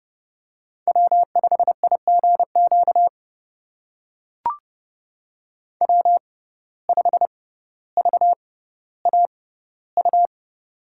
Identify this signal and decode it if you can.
Morse code, heard as a keyed tone, about 30 wpm, spaced as for 15 wpm, 700 Hz. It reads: W5IGQ W5VAU
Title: USA Call Signs - 5 Characters - 6x Character Spacing - 30wpm